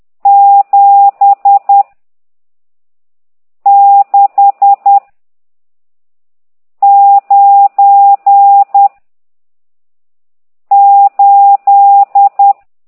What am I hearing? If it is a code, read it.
Morse code, heard as a keyed tone, about 10 wpm, 800 Hz. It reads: 7 6 9 8
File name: ct2idw / exerc14